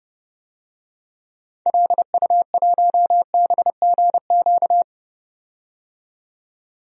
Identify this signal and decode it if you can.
Morse code, heard as a keyed tone, about 30 wpm, 700 Hz. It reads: LU1BGQ
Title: DXCC-100 Call Signs - 6 Characters - 30wpm